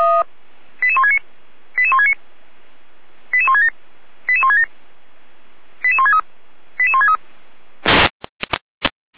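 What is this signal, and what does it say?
Five-tone selective call (ZVEI), heard as quick runs of five tones, tones of about 70 ms. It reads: 88169 88169 88167 88167 88163 88163
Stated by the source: ZVEI five-tone call received on SDR.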